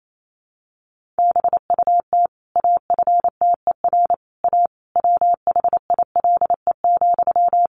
Morse code, heard as a keyed tone, about 28 wpm, 700 Hz. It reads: BUT AFTER A WHILE,
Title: Sentences from Top 1500 Words 28wpm